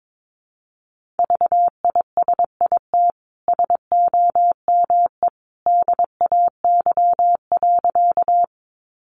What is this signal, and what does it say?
Morse code, heard as a keyed tone, about 22 wpm, 700 Hz. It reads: VISIT SOME DAY.
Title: Sentences from Top 900 Words 22wpm